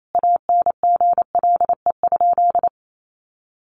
Morse code, heard as a keyed tone, about 28 wpm, 700 Hz. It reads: ANGLE?
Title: Sentences from Top 1500 Words 28wpm